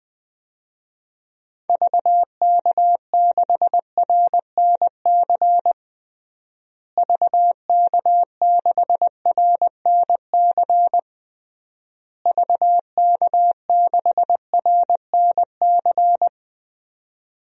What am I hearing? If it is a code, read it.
Morse code, heard as a keyed tone, about 20 wpm, 700 Hz. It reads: VK6RNC VK6RNC VK6RNC
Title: DXCC-100 Call Signs - 6 Characters - Repeated 3x - 20wpm